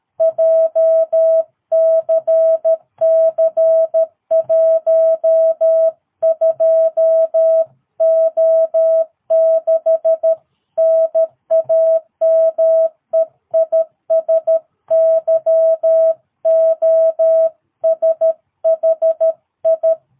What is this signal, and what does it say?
Morse code, heard as a keyed tone, about 13 wpm, 650 Hz. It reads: JCC12O6 NAME IS YOSHI
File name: jh8cbh / 2022cw